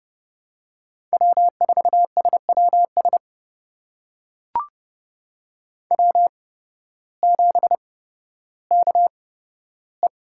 Morse code, heard as a keyed tone, about 30 wpm, spaced as for 14 wpm, 700 Hz. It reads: W4SWS W7KE
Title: USA Call Signs - 5 Characters - 8x Character Spacing - 30wpm